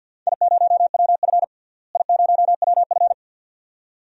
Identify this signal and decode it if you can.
Morse code, heard as a keyed tone, about 50 wpm, 700 Hz. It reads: I0WF I0WF
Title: DXCC-100 Call Signs - 4 Characters - Repeated 2x - 50wpm